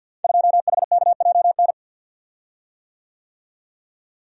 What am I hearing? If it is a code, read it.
Morse code, heard as a keyed tone, about 52 wpm, 700 Hz. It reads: 2HKJN
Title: Code Groups - 5 Letter:Numbers - 50wpm